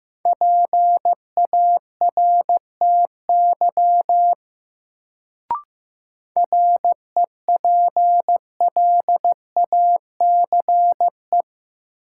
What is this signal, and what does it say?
Morse code, heard as a keyed tone, about 16 wpm, 700 Hz. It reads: PARTY REPLACE
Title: Sentences from Top 1400 Words - New Word Drill - 15wpm